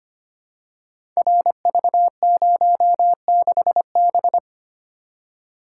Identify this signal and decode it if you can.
Morse code, heard as a keyed tone, about 25 wpm, 700 Hz. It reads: RV06B
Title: Code Groups - 5 Letter:Numbers - 25wpm